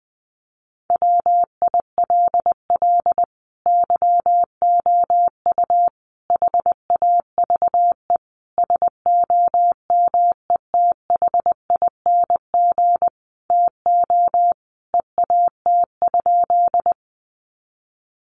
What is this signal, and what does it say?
Morse code, heard as a keyed tone, about 20 wpm, 700 Hz. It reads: WILL YOU HAVE SOMETHING TO EAT?
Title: Sentences from Top 600 Words 20wpm